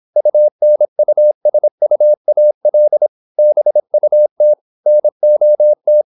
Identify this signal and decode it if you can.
Morse code, heard as a keyed tone, about 26 wpm, 600 Hz. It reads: UNUSUAL BUT NOT
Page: Never 26wpm | CW med Gnister